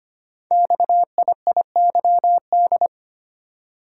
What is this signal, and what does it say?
Morse code, heard as a keyed tone, about 25 wpm, 700 Hz. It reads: XIIYD
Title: Code Groups - 5 Letter:Numbers - 25wpm